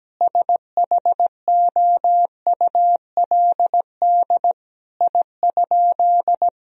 Morse code, heard as a keyed tone, about 17 wpm, 700 Hz. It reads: SHOULD I?
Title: Sentences from Top 1000 Words - Review Part 1 - 17wpm